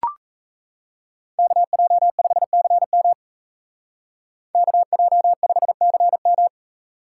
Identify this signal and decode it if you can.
Morse code, heard as a keyed tone, about 34 wpm, 700 Hz. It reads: KJ5CM KJ5CM
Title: Speed Racing - USA 2x2 Call Signs 28wpm